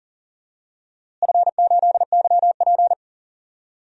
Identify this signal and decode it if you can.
Morse code, heard as a keyed tone, about 40 wpm, 700 Hz. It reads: F8YP
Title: DXCC-100 Call Signs - 4 Characters - Repeated 2x - 40wpm